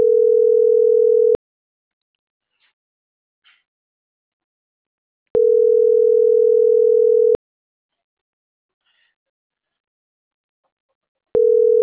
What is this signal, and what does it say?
Ringback tone (North America), 440 + 480 Hz.